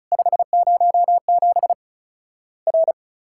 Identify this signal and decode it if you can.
Morse code, heard as a keyed tone, about 36 wpm, 700 Hz. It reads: 507 R
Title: Worked All Europe DX Contest - QTC Exchange - 35wpm